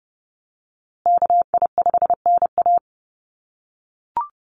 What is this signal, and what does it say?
Morse code, heard as a keyed tone, about 30 wpm, 700 Hz. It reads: KI5NA
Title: USA Call Signs Repeated 3x 30wpm